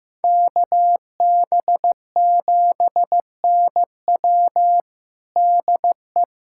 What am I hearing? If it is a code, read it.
Morse code, heard as a keyed tone, about 16 wpm, 700 Hz. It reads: KB7NW DE
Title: USA Callsign DE Callsign - 5 Characters - 15wpm